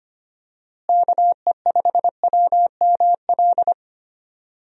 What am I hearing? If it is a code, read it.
Morse code, heard as a keyed tone, about 25 wpm, 700 Hz. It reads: KE5WML
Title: USA Call Signs - 6 Characters - Repeated 2x - 25wpm